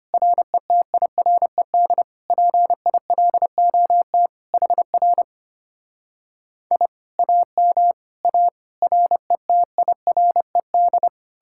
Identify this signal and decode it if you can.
Morse code, heard as a keyed tone, about 33 wpm, 700 Hz. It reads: RETIRED PILOT HR I AM A RETIRED